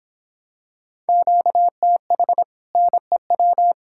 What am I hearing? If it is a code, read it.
Morse code, heard as a keyed tone, about 27 wpm, 700 Hz. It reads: QTH NEW